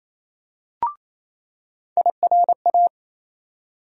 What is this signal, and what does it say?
Morse code, heard as a keyed tone, about 27 wpm, 700 Hz. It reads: IRA
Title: USA Call Sign Postfixes - 3 Letters 28wpm